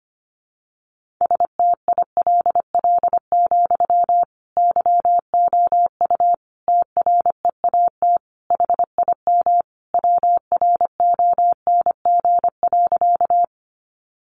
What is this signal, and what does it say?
Morse code, heard as a keyed tone, about 25 wpm, 700 Hz. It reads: STILL, YOU TREAT HIM WRONG.
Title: Sentences from Top 1300 Words 25wpm